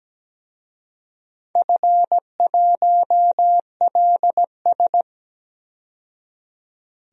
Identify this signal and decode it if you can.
Morse code, heard as a keyed tone, about 17 wpm, 700 Hz. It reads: F1LS
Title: DXCC-100 Call Signs - 4 Characters - 17wpm